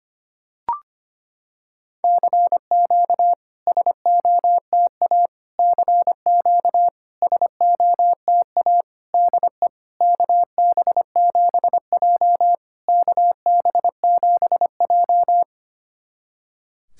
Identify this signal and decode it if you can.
Morse code, heard as a keyed tone, about 25 wpm, 690 Hz. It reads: CQ SOTA CQ SOTA DE KB7J KB7J
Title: SOTA QSO Practice 25wpm